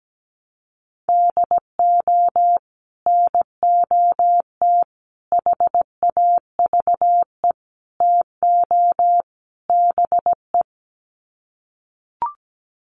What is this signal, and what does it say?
Morse code, heard as a keyed tone, about 17 wpm, 700 Hz. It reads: DO NOT HAVE TO BE
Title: Sets of 5 Words 17wpm